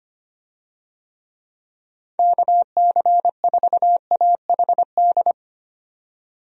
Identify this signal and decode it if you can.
Morse code, heard as a keyed tone, about 25 wpm, 700 Hz. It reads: KC4AHD
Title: USA Call Signs - 6 Characters - 25wpm